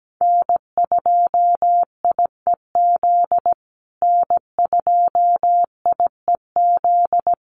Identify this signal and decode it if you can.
Morse code, heard as a keyed tone, about 17 wpm, 700 Hz. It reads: N2IEZ N2IEZ
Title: USA Call Signs Repeated 3x 17wpm